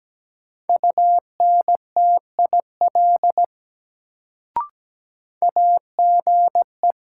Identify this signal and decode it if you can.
Morse code, heard as a keyed tone, about 17 wpm, 700 Hz. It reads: UNTIL AGE
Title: Sentences from Top 500 Words - New Word Drill - 17wpm